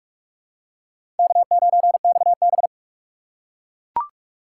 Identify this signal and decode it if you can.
Morse code, heard as a keyed tone, about 41 wpm, 700 Hz. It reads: K9XB
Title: USA Call Signs - 4 Characters - 2x Character Spacing - 45wpm